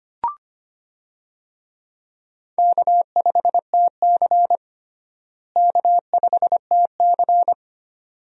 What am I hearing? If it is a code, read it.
Morse code, heard as a keyed tone, about 25 wpm, 700 Hz. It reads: K5TC K5TC
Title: USA Call Signs - 4 Characters - Repeated 2x - 25wpm